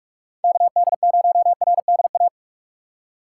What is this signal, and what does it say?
Morse code, heard as a keyed tone, about 46 wpm, 700 Hz. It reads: KD0RDA
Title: USA Call Signs - 4x Character Spacing - 45wpm